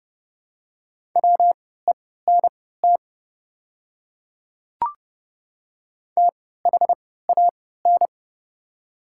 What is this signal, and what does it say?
Morse code, heard as a keyed tone, about 30 wpm, spaced as for 17 wpm, 700 Hz. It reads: WENT THAN
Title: 3x Character Spacing - Top 200 Words - 30wpm